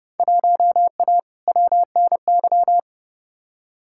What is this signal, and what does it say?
Morse code, heard as a keyed tone, about 31 wpm, 700 Hz. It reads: 1A WNY
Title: Field Day Exchanges - 30wpm